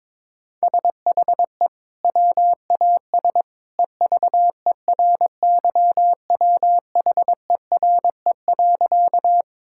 Morse code, heard as a keyed tone, about 22 wpm, 700 Hz. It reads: SHE WAS EVERYWHERE.
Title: Sentences from Top 1600 Words 22wpm